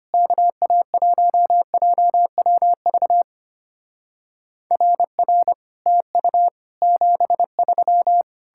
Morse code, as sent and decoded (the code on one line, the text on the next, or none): -.- .- .---- .--- .-- ...- / .-. .-. / - ..- / --... ...--
KA1JWV RR TU 73